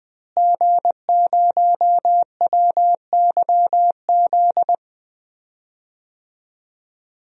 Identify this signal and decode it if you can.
Morse code, heard as a keyed tone, about 21 wpm, 700 Hz. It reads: G0WYZ
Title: DXCC-100 Call Signs - 5 Characters - Repeated 3x - 20wpm